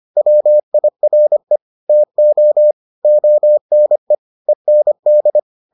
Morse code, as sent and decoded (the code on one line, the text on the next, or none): .-- .. .-. / - --- / --- -. / . -. -..
WIRE TO ONE END